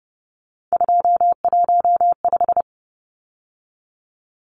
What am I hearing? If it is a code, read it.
Morse code, heard as a keyed tone, about 30 wpm, 700 Hz. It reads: 215
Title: Numbers - 3-Digit 30wpm